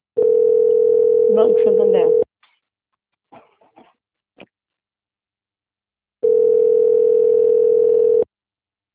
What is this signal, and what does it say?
Ringback tone (North America), 440 + 480 Hz, with speech.
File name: out- - - - - .4276